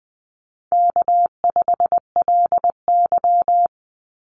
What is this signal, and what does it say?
Morse code, heard as a keyed tone, about 20 wpm, 700 Hz. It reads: K5LY
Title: Call Signs - 20wpm